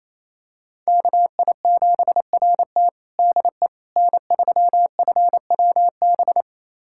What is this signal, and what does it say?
Morse code, heard as a keyed tone, about 28 wpm, 700 Hz. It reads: KI7RT DE N3FWB
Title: USA Callsign DE Callsign - 5 Characters - 28wpm